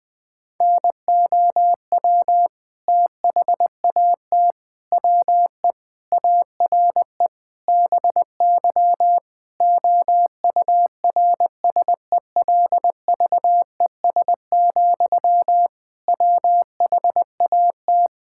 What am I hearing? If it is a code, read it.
Morse code, heard as a keyed tone, about 21 wpm, 700 Hz. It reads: NOW THAT WE ARE BY OURSELVES, WHAT
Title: Sentences from Top 1600 Words 20wpm